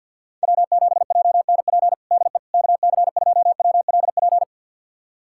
Morse code, heard as a keyed tone, about 46 wpm, 700 Hz. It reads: W7JNP DE KC2WLP
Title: USA Callsign DE Callsign 50wpm